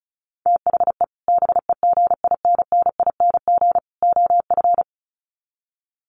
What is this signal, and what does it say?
Morse code, heard as a keyed tone, about 34 wpm, 700 Hz. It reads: THE BEGINNING OF
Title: Sets of 3 Words 35wpm